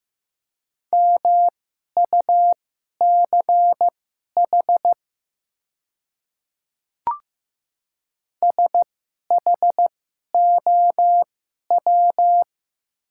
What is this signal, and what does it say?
Morse code, heard as a keyed tone, about 15 wpm, spaced as for 11 wpm, 700 Hz. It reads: MUCH SHOW
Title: 2x Character Spacing - Top 300 Words - 15wpm